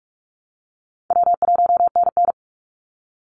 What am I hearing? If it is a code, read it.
Morse code, heard as a keyed tone, about 44 wpm, 700 Hz. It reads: W1NN
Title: Easy Call Signs Repeated 3x - 45wpm